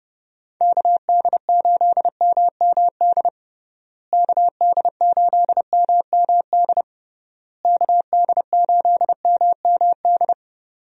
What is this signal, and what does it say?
Morse code, heard as a keyed tone, about 30 wpm, 700 Hz. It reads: KD8MMD KD8MMD KD8MMD
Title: USA Call Signs - 6 Characters - Repeated 3x - 30wpm